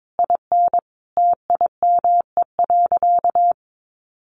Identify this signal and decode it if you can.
Morse code, heard as a keyed tone, about 22 wpm, 700 Hz. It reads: IN TIME.
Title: Sentences from Top 1400 Words 22wpm